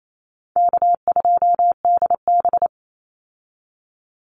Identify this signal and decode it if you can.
Morse code, heard as a keyed tone, about 28 wpm, 700 Hz. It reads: K2DB
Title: Easy Call Signs Repeated 3x - 28wpm